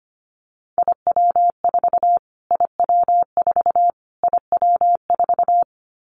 Morse code, heard as a keyed tone, about 25 wpm, 700 Hz. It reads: IW4 IW4 IW4
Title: Callsign Prefixes for Top 25 DXCC - Repeated 3x - 25wpm